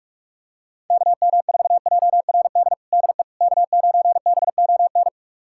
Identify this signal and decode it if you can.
Morse code, heard as a keyed tone, about 46 wpm, 700 Hz. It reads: KM4JRD DE K9BON
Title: USA Callsign DE Callsign 45wpm